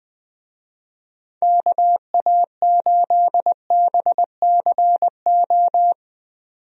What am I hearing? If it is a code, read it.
Morse code, heard as a keyed tone, about 20 wpm, 700 Hz. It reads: KA8BCO
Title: USA Call Signs - 6 Characters - 20wpm